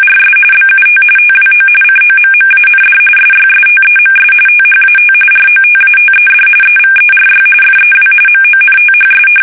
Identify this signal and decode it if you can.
RTTY, about 45.45 baud. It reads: RS BUT TWO COMMAND SYSTEMS THAT ROTATED EVERY FEW MONTHS
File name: fsk850